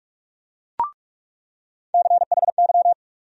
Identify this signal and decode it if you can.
Morse code, heard as a keyed tone, about 46 wpm, 700 Hz. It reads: CHY
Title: USA Call Sign Postfixes - 3 Letters 45wpm